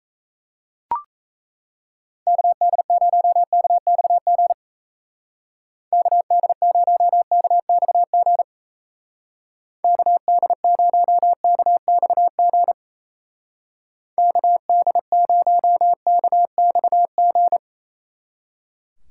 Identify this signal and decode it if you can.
Morse code, heard as a keyed tone, about 37 wpm, 700 Hz. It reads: KD0KXG KD0KXG KD0KXG KD0KXG